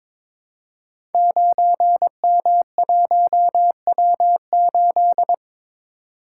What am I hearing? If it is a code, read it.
Morse code, heard as a keyed tone, about 23 wpm, 700 Hz. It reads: 9M1W8
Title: Code Groups - 5 Letter:Numbers - 22wpm